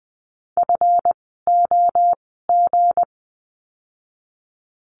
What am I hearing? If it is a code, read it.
Morse code, heard as a keyed tone, about 20 wpm, 700 Hz. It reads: F O G
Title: 160 Lesson 26f - Letter g - Words 20wpm